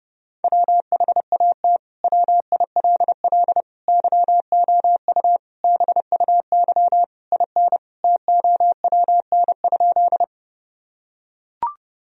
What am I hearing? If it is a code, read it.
Morse code, heard as a keyed tone, about 30 wpm, 700 Hz. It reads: WHAT WILL YOU BUY IN TOWN?
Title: Sentences from Top 1000 Words - Review Part 4 - 30wpm